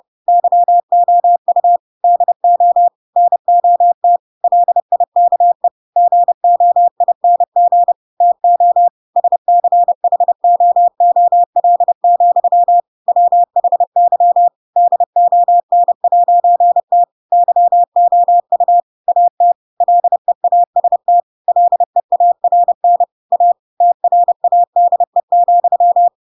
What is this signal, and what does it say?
Morse code, heard as a keyed tone, about 30 wpm, 700 Hz. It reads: YOU DO NOT LIKE GOING TO SCHOOL, WHY DON'T YOU AT LEAST LEARN A TRADE,
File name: Pinocchio / Pinocchio ch4 30 30 ik1hge